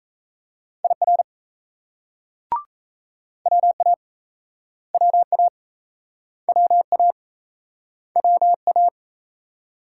Morse code, heard as a keyed tone, about 38 wpm, 700 Hz. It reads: IR WA WA WA WA